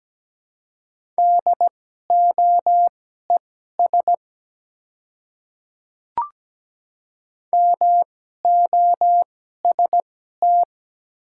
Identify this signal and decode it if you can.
Morse code, heard as a keyed tone, about 17 wpm, spaced as for 12 wpm, 700 Hz. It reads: DOES MOST
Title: 2x Character Spacing - Top 200 Words - 17wpm